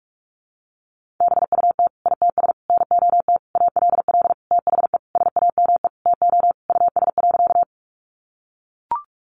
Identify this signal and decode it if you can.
Morse code, heard as a keyed tone, about 44 wpm, 700 Hz. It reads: BUT ITS NOT ALL THE SAME TO US.
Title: Sentences from Top 600 Words 45wpm